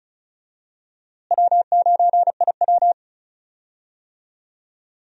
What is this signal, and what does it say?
Morse code, heard as a keyed tone, about 36 wpm, 700 Hz. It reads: W9IW
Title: USA Call Signs - 4 Characters - Repeated 2x - 35wpm